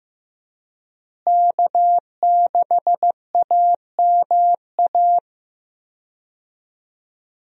Morse code, heard as a keyed tone, about 15 wpm, 700 Hz. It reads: K6AMA
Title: USA Call Signs - 5 Characters - Repeated 2x - 15wpm